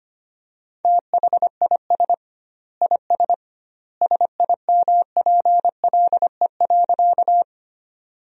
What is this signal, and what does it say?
Morse code, heard as a keyed tone, about 25 wpm, 700 Hz. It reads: THIS IS SIMPLE.
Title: 2x Word-Spacing - Sentences from Top 400 Words - 25wpm